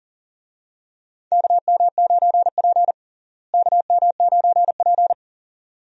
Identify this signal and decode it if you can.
Morse code, heard as a keyed tone, about 40 wpm, 700 Hz. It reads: KM9P KM9P
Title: USA Call Signs - 4 Characters - Repeated 2x - 40wpm